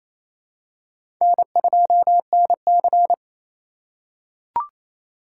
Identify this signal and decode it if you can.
Morse code, heard as a keyed tone, about 28 wpm, 700 Hz. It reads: N2NC
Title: K1USN SST QSO Practice - 28wpm